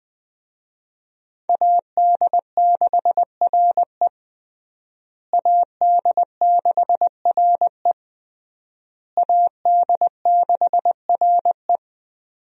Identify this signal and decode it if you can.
Morse code, heard as a keyed tone, about 20 wpm, 700 Hz. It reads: AD6RE AD6RE AD6RE
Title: USA Call Signs - 5 Characters - Repeated 3x - 20wpm